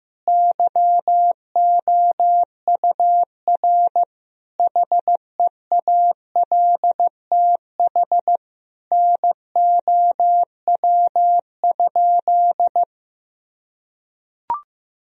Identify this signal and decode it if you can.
Morse code, heard as a keyed tone, about 16 wpm, 700 Hz. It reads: YOUR HEALTH NOW?
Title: Sentences from Top 1000 Words - Review Part 3 - 15wpm